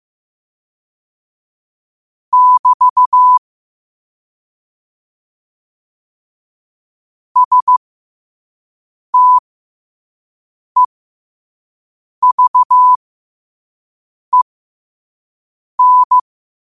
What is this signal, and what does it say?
Morse code, heard as a keyed tone, about 15 wpm, spaced as for 5 wpm, 1000 Hz. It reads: <BT> STEVEN
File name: qotdfiles / 5wpm